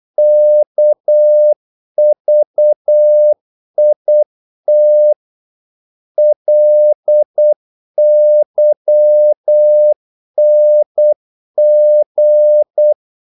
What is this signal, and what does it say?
Morse code, heard as a keyed tone, about 8 wpm, 600 Hz. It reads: KVIT LYNG
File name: Korte ord DK 8wpm